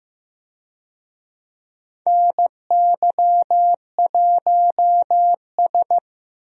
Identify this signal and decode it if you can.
Morse code, heard as a keyed tone, about 15 wpm, 700 Hz. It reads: NY1S
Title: USA Call Signs - 4 Characters - Repeated 3x - 15wpm